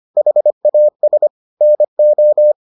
Morse code, heard as a keyed tone, about 25 wpm, 600 Hz. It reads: HAS NO